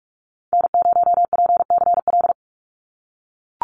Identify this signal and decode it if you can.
Morse code, heard as a keyed tone, about 44 wpm, 700 Hz. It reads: N0PCL
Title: NA SOTA Activator Callsigns 45wpm